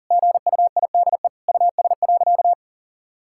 Morse code, heard as a keyed tone, about 40 wpm, 700 Hz. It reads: GUIDE US.
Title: Sentences from Top 1000 Words - Review Part 5 - 40wpm